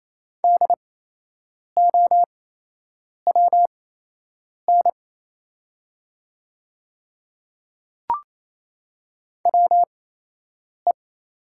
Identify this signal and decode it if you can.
Morse code, heard as a keyed tone, about 28 wpm, spaced as for 8 wpm, 700 Hz. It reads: DOWN WE
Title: 8x Character Spacing - Top 100 Words - 28wpm